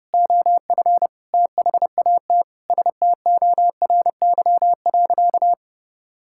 Morse code, heard as a keyed tone, about 30 wpm, 700 Hz. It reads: OF THAT STORY.
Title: Sentences from Top 1000 Words - Review Part 5 - 30wpm